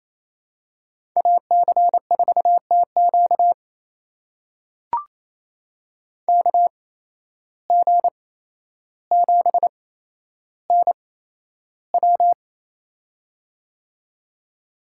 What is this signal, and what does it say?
Morse code, heard as a keyed tone, about 28 wpm, spaced as for 12 wpm, 700 Hz. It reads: AC4TQ KG7NW
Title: USA Call Signs - 5 Characters - 8x Character Spacing - 28wpm